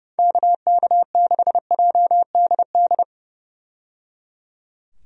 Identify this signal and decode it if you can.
Morse code, heard as a keyed tone, about 30 wpm, 700 Hz. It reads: KK6JDD
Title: USA Call Signs - 6 Characters - 30wpm